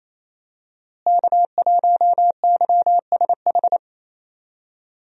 Morse code, heard as a keyed tone, about 28 wpm, 700 Hz. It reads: K1YSH
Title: USA Call Signs - 5 Characters - Repeated 2x - 28wpm